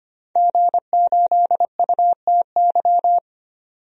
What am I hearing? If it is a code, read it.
Morse code, heard as a keyed tone, about 25 wpm, 700 Hz. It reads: G8UTY
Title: DXCC-100 Call Signs - 5 Characters - 25wpm